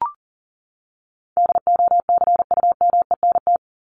Morse code, heard as a keyed tone, about 37 wpm, 700 Hz. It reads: DOCUMENT
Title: Top 1751-2000 Words 40wpm